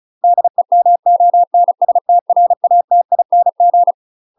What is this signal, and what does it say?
Morse code, heard as a keyed tone, about 33 wpm, 700 Hz. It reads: DEMONSTRATING
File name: UK MASTER QSOs TRAINING / 35WPM / UK MASTER QSOs 0002